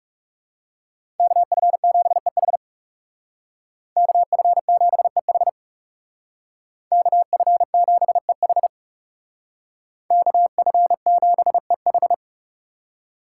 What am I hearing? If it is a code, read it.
Morse code, heard as a keyed tone, about 37 wpm, 700 Hz. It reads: KF7EH KF7EH KF7EH KF7EH
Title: Speed Racing - USA 2x2 Call Signs 30wpm